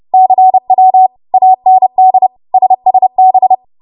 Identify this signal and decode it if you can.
Morse code, heard as a keyed tone, about 30 wpm, 750 Hz. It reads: CW AND SSB
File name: Morse / 30WPM